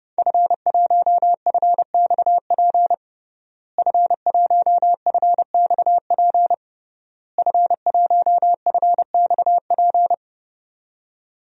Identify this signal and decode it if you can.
Morse code, heard as a keyed tone, about 30 wpm, 700 Hz. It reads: F1FXP F1FXP F1FXP
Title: DXCC-100 Call Signs - 5 Characters - Repeated 3x - 30wpm